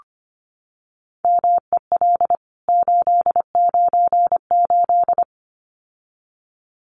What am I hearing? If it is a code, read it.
Morse code, heard as a keyed tone, about 25 wpm, 700 Hz. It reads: MEL 898
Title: CWT Exchange 25wpm